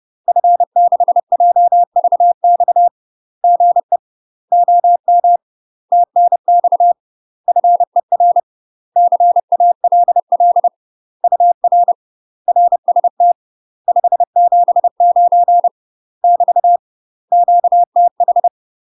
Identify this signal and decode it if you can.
Morse code, heard as a keyed tone, about 30 wpm, 700 Hz. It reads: F6JVX GE OM TNX FER CALL UR RST 579 <BT> QTH
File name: UK MASTER QSOs TRAINING / 30WPM / UK MASTER QSOs 0024